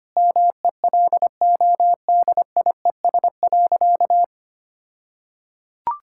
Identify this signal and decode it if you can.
Morse code, heard as a keyed tone, about 25 wpm, 700 Hz. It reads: MELODIES.
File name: practice / 25wpm / All-Age Appropriate Jokes - 25wpm